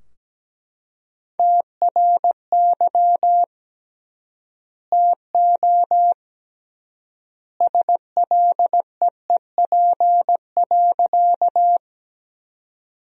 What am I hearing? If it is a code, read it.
Morse code, heard as a keyed tone, about 17 wpm, 700 Hz. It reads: TRY TO SLEEP.
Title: 3x Word-Spacing - Sentences from Top 500 Words - 17wpm